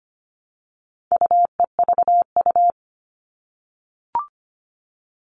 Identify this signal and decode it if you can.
Morse code, heard as a keyed tone, about 25 wpm, 700 Hz. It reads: UEVU
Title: All Cut Numbers - 4-Digit 25wpm